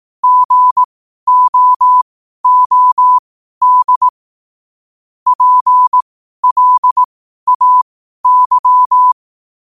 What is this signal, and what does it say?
Morse code, heard as a keyed tone, about 18 wpm, spaced as for 12 wpm, 1000 Hz. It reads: GOOD PLAY
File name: qotdfiles / 12wpm